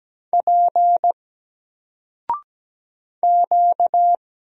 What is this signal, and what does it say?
Morse code, heard as a keyed tone, about 17 wpm, 700 Hz. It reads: P Q
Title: Single Character (Letter:Number:Punctuation:Prosign) 17wpm